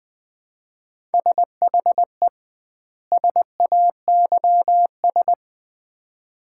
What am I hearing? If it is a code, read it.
Morse code, heard as a keyed tone, about 20 wpm, 700 Hz. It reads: SHE SAYS
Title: 2x Word-Spacing - Sets of 2 Words - 20wpm